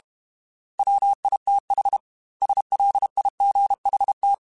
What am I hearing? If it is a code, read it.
Morse code, heard as a keyed tone, about 32 wpm, 800 Hz. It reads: WITH SLIGHT
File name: Weather Sets / Weather Sets-mp3s / Weather 4 17 @30